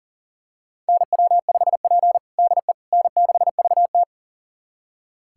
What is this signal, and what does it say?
Morse code, heard as a keyed tone, about 40 wpm, 690 Hz. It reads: NW5P DE N6VT
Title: USA Callsign DE Callsign - 4 Characters - 40wpm